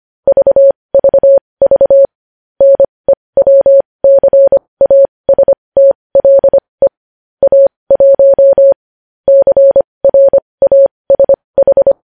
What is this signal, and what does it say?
Morse code, heard as a keyed tone, about 25 wpm, spaced as for 20 wpm, 550 Hz. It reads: VVV NEWCASTLE A1 CRASH
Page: news headlines in morse code at 20 wpm